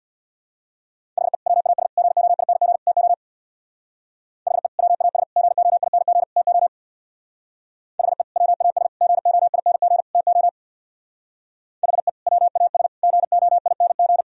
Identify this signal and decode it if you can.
Morse code, heard as a keyed tone, about 64 wpm, 700 Hz. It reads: HE WAS GOING TO HE WAS GOING TO HE WAS GOING TO HE WAS GOING